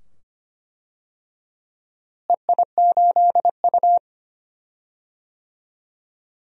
Morse code, heard as a keyed tone, about 25 wpm, 700 Hz. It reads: EI8U
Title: DXCC-100 Call Signs - 4 Characters - Repeated 2x - 25wpm